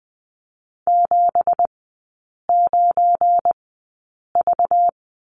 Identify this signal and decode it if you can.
Morse code, heard as a keyed tone, about 20 wpm, spaced as for 8 wpm, 690 Hz. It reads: 79V
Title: 191 Lesson 31e - Letter v - Three Characters 20wpm